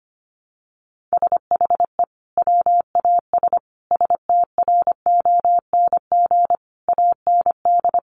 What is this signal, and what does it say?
Morse code, heard as a keyed tone, about 25 wpm, 700 Hz. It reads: SHE WAS STRONG AND
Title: Sentences from Top 1200 Words 25wpm